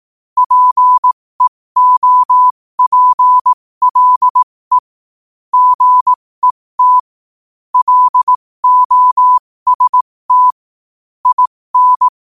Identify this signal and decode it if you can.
Morse code, heard as a keyed tone, about 18 wpm, spaced as for 15 wpm, 1000 Hz. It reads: PEOPLE GET LOST IN
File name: qotdfiles / 15wpm